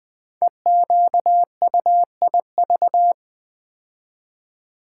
Morse code, heard as a keyed tone, about 20 wpm, 700 Hz. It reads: EQUIV
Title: Build A Word - 5 Letters At A Time - 10-Letter Words - 20wpm